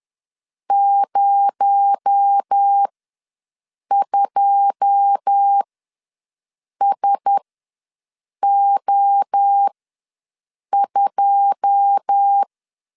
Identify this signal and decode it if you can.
Morse code, heard as a keyed tone, about 10 wpm, spaced as for 6 wpm, 800 Hz. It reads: 02SO2
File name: LEZIONI / 9 2 ws 10wpm 4130